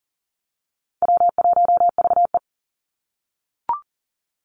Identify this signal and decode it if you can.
Morse code, heard as a keyed tone, about 40 wpm, 700 Hz. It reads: W1VE
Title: Call Signs - 40wpm